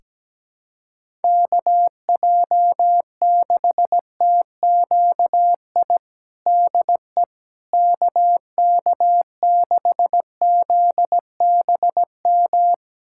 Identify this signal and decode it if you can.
Morse code, heard as a keyed tone, about 17 wpm, 700 Hz. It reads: KJ6TQI DE KK6ZBM
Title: USA Callsign DE Callsign 17wpm